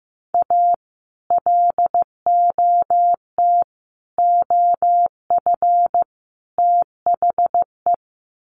Morse code, heard as a keyed tone, about 15 wpm, 700 Hz. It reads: A LOT OF THE
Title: Sets of 4 Words 15wpm